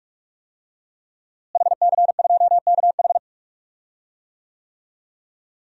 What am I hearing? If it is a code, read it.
Morse code, heard as a keyed tone, about 45 wpm, 700 Hz. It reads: HC2KH
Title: DXCC-100 Call Signs - 5 Characters - Repeated 2x - 45wpm